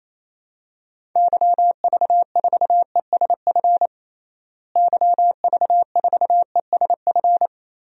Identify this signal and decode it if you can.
Morse code, heard as a keyed tone, about 28 wpm, 700 Hz. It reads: YV4ESF YV4ESF
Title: DXCC-100 Call Signs - 6 Characters - Repeated 3x - 28wpm